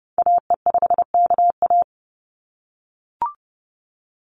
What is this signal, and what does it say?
Morse code, heard as a keyed tone, about 30 wpm, 700 Hz. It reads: AE5KA
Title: NA SOTA Activator Callsigns 30wpm